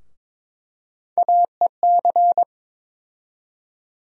Morse code, heard as a keyed tone, about 22 wpm, 700 Hz. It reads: AEC
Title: USA Call Sign Postfixes - 3 Letters 22wpm